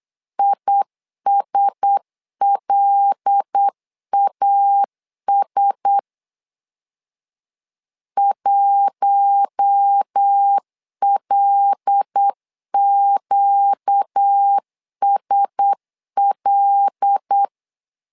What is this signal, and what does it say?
Morse code, heard as a keyed tone, about 8 wpm, 800 Hz. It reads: ISLAS 1LQSL
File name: LEZIONI / 5 LBVC / 5 L 8wpm 2831